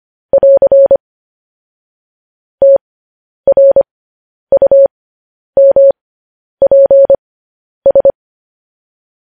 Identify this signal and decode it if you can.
Morse code, heard as a keyed tone, about 25 wpm, spaced as for 10 wpm, 550 Hz. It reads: <AR> TRUMPS
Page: news-headlines-in-morse-code-at-10-wpm